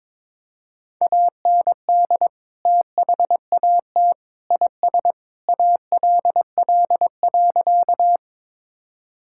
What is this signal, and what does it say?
Morse code, heard as a keyed tone, about 22 wpm, 700 Hz. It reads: AND THAT IS ALL.
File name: practice / 22wpm / Sentences with 3-Word Phrases - 22wpm